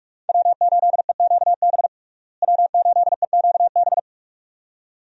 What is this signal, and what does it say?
Morse code, heard as a keyed tone, about 44 wpm, 700 Hz. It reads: W8EQB W8EQB
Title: USA Call Signs - 5 Characters - Repeated 2x - 45wpm